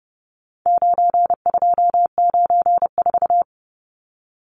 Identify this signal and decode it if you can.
Morse code, heard as a keyed tone, about 30 wpm, 700 Hz. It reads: 9294
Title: Numbers - 4-Digit 30wpm